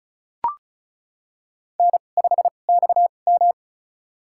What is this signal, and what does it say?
Morse code, heard as a keyed tone, about 36 wpm, spaced as for 25 wpm, 700 Hz. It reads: N5XM
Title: USA Call Signs - 4 Characters - 2x Character Spacing - 35wpm